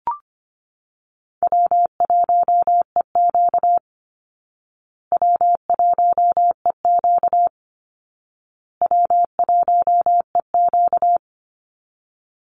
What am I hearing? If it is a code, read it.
Morse code, heard as a keyed tone, about 24 wpm, 700 Hz. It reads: W1EQ W1EQ W1EQ
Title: Easy Call Signs Repeated 3x - 25wpm